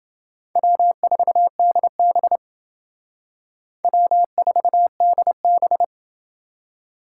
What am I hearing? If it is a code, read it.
Morse code, heard as a keyed tone, about 28 wpm, 700 Hz. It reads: W4DB W4DB